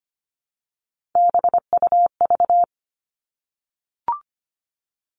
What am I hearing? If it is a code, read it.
Morse code, heard as a keyed tone, about 25 wpm, 700 Hz. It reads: BUV
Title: All Cut Numbers - 3-Digit 25wpm